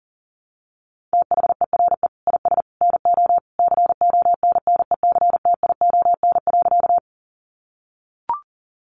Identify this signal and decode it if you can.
Morse code, heard as a keyed tone, about 40 wpm, 700 Hz. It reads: THERE IS NO CONNECTION.
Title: Sentences from Top 1300 Words 40wpm